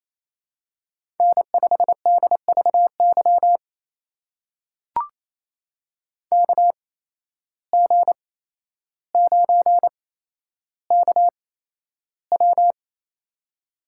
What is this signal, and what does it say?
Morse code, heard as a keyed tone, about 28 wpm, spaced as for 12 wpm, 700 Hz. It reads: N5DVY KG9KW
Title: USA Call Signs - 5 Characters - 8x Character Spacing - 28wpm